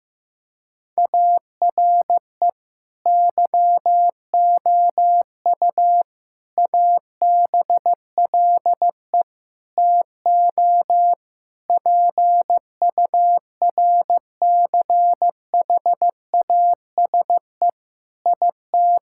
Morse code, heard as a keyed tone, about 15 wpm, 700 Hz. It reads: ARE YOU ABLE TO PURCHASE IT
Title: Sentences from Top 1000 Words - Review Part 5 - 15wpm